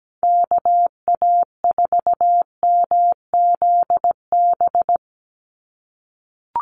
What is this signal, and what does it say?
Morse code, heard as a keyed tone, about 17 wpm, 700 Hz. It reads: KA4MZB
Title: USA Call Signs 17wpm